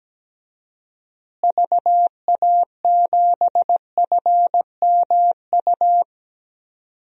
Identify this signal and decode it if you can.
Morse code, heard as a keyed tone, about 17 wpm, 700 Hz. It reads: VA7FMU
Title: DXCC-100 Call Signs - 6 Characters - Repeated 2x - 17wpm